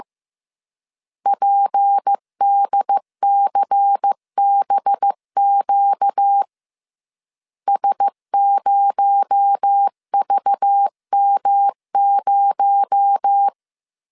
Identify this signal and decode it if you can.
Morse code, heard as a keyed tone, about 15 wpm, 800 Hz. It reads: PDCBQ S0VM0
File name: CW / DET / 1 6 1500C S B 14wpm 2958